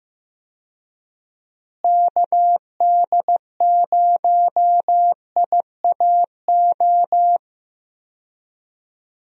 Morse code, heard as a keyed tone, about 15 wpm, 700 Hz. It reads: KD0IAO
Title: USA Call Signs - 6 Characters - Repeated 3x - 15wpm